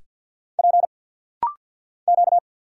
Morse code, heard as a keyed tone, about 50 wpm, 700 Hz. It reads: P Q
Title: Single Letters - Rapid-Fire 50wpm